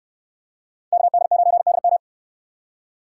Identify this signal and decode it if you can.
Morse code, heard as a keyed tone, about 59 wpm, 700 Hz. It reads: KD9DM